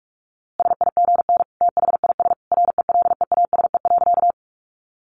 Morse code, heard as a keyed tone, about 44 wpm, 700 Hz. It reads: SIGN THIS RELEASE.
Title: A Sound Of Thunder 45wpm